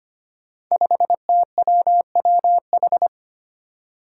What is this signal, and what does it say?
Morse code, heard as a keyed tone, about 25 wpm, 700 Hz. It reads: 5TWWH
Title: Five Challenging Letter-Number Combinations 25wpm